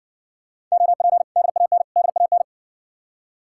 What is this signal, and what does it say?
Morse code, heard as a keyed tone, about 66 wpm, 700 Hz. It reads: OP DAN DAN